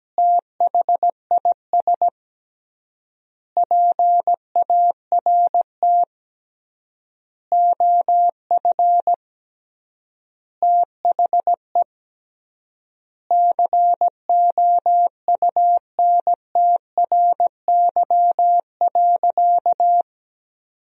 Morse code, heard as a keyed tone, about 17 wpm, 700 Hz. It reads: THIS PART OF THE COUNTRY.
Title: 3x Word-Spacing - Sentences from Top 300 Words - 17wpm